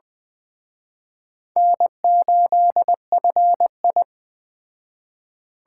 Morse code, heard as a keyed tone, about 21 wpm, 700 Hz. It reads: N8FI
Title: Speed Racing - USA 1x2 Call Signs 17wpm